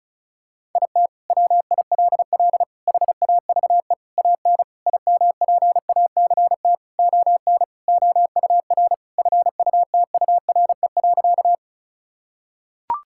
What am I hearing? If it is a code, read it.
Morse code, heard as a keyed tone, about 36 wpm, 700 Hz. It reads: IT WILL HAVE AN IMPACT ON OUR FUTURE.
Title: Sentences from Top 1000 Words - Review Part 6 - 35wpm